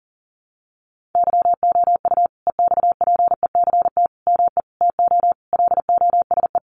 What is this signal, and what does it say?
Morse code, heard as a keyed tone, about 40 wpm, 700 Hz. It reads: YOU EXPECT ME TO LOSE
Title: Sentences from Top 1100 Words 40wpm